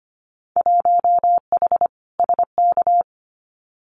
Morse code, heard as a keyed tone, about 25 wpm, 700 Hz. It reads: 1H SK
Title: Winter Field Day Exchanges 25wpm